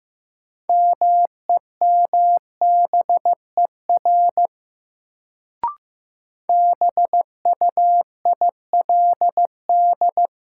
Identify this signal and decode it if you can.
Morse code, heard as a keyed tone, about 15 wpm, 700 Hz. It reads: MEMBER BUILD
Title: Sentences from Top 1100 Words - New Word Drill Alt - 15wpm